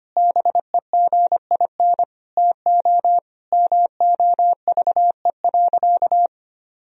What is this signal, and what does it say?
Morse code, heard as a keyed tone, about 25 wpm, 700 Hz. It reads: BEGIN TO MOVE.
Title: Sentences from Top 1000 Words - Review Part 2 - 25wpm